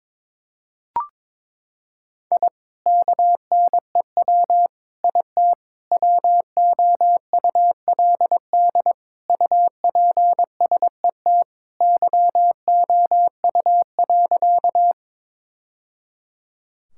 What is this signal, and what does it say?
Morse code, heard as a keyed tone, about 22 wpm, 700 Hz. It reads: I KNEW IT WOULD UPSET YOU.
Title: Sentences from Top 1000 Words - Review Part 5 - 22wpm